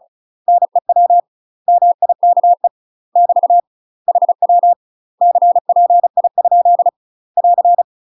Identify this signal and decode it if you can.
Morse code, heard as a keyed tone, about 36 wpm, 700 Hz. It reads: NEW MIKE <BT> HW CPI? <AR>
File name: UK MASTER QSOs TRAINING / 35WPM / UK MASTER QSOs 0013